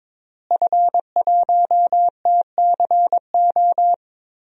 Morse code, heard as a keyed tone, about 22 wpm, 700 Hz. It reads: F1TCO
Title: DXCC-100 Call Signs - 5 Characters - Repeated 3x - 22wpm